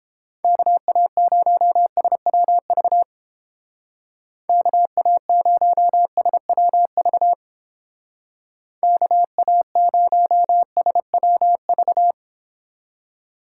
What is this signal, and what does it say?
Morse code, heard as a keyed tone, about 28 wpm, 700 Hz. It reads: KA0SWV KA0SWV KA0SWV